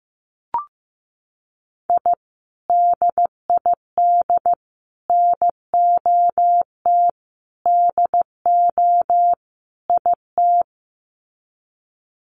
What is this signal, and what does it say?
Morse code, heard as a keyed tone, about 15 wpm, 700 Hz. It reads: I DID NOT DO IT
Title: Sets of 5 Words Encore2 15wpm